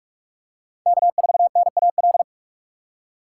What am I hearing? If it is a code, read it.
Morse code, heard as a keyed tone, about 43 wpm, 700 Hz. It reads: K4NAL